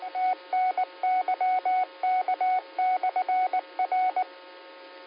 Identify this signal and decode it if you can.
Morse code, heard as a keyed tone, about 19 wpm, 740 Hz, with radio noise. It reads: TNYK/R
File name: qsorec SimplexLogic